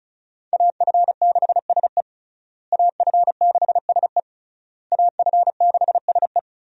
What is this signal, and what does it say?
Morse code, heard as a keyed tone, about 36 wpm, 700 Hz. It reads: AF6SE AF6SE AF6SE
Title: USA Call Signs - 5 Characters - Repeated 3x - 35wpm